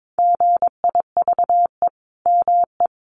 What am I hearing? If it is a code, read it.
Morse code, heard as a keyed tone, about 22 wpm, 700 Hz. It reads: GIVE ME
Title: Sentences from Top 800 Words 22wpm